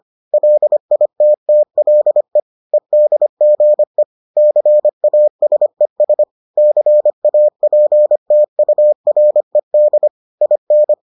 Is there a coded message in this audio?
Yes: Morse code from 0.3 to 10.9 s, heard as a keyed tone, about 25 wpm, 600 Hz. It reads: LITTLE EDGE CASES CAPTURED IN